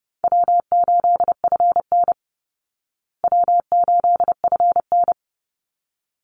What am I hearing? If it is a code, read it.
Morse code, heard as a keyed tone, about 30 wpm, 700 Hz. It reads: W8FN W8FN
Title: Easy Call Signs Repeated 2x - 30wpm